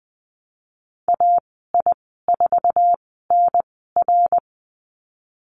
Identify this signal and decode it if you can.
Morse code, heard as a keyed tone, about 20 wpm, spaced as for 14 wpm, 700 Hz. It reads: AI4NR
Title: 055 Lesson 10g - Letter r - Call Signs 20wpm